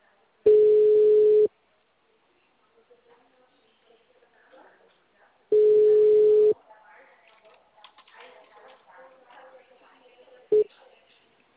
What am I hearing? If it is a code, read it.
Ringback tone (Europe, 425 Hz), with speech.